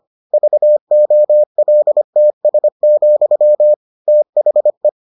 Morse code, heard as a keyed tone, about 25 wpm, 600 Hz. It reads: VOLTS, THE